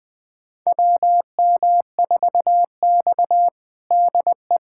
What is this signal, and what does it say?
Morse code, heard as a keyed tone, about 20 wpm, 700 Hz. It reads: WM4X DE
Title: USA Callsign DE Callsign - 4 Characters - 20wpm